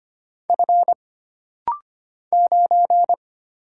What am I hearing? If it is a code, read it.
Morse code, heard as a keyed tone, about 25 wpm, 700 Hz. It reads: F 9
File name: Single Character (Letter:Number:Punctuation:Prosign) - Rapid-Fire - 25wpm